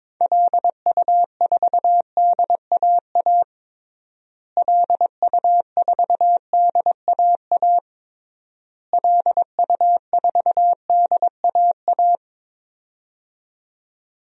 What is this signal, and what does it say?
Morse code, heard as a keyed tone, about 22 wpm, 700 Hz. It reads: LU4DAA LU4DAA LU4DAA
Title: DXCC-100 Call Signs - 6 Characters - Repeated 3x - 22wpm